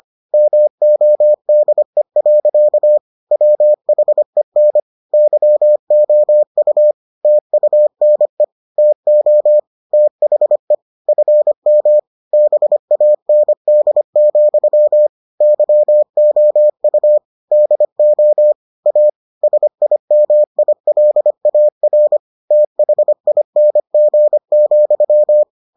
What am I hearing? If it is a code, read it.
Morse code, heard as a keyed tone, about 25 wpm, 600 Hz. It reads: MODE. WHEN YOU TUNE TO THE FM BAND, YOU DO A SIMILAR THING,